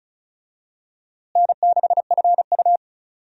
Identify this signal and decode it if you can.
Morse code, heard as a keyed tone, about 36 wpm, 700 Hz. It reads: N6FU
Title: USA Call Signs - 4 Characters - Repeated 3x - 35wpm